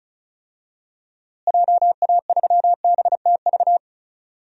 Morse code, heard as a keyed tone, about 34 wpm, 700 Hz. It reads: JA3BTV
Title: DXCC-100 Call Signs - 6 Characters - Repeated 3x - 35wpm